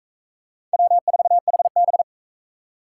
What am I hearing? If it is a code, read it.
Morse code, heard as a keyed tone, about 41 wpm, 700 Hz. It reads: W4HB